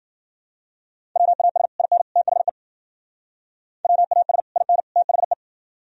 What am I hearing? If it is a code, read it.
Morse code, heard as a keyed tone, about 52 wpm, 700 Hz. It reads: WAS IN THE WAS IN THE